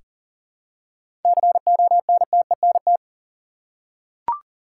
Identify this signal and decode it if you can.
Morse code, heard as a keyed tone, about 38 wpm, 700 Hz. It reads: CONTENT
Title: Seven Letter Words - 40wpm